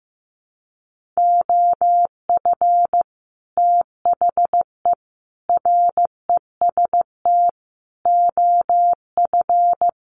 Morse code, heard as a keyed tone, about 15 wpm, 700 Hz. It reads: OF THE REST OF
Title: Sets of 5 Words 15wpm